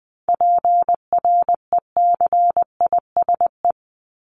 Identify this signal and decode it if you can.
Morse code, heard as a keyed tone, about 20 wpm, 700 Hz. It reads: PRECISE
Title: LY Suffix - Top 100 Words with Base Words 20wpm